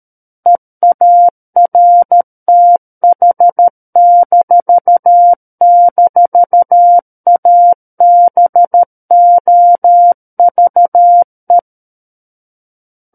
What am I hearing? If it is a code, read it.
Morse code, heard as a keyed tone, about 13 wpm, 700 Hz. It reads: EARTH--ABOVE
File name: Book One / 04-Chapter 4 - 13 WPM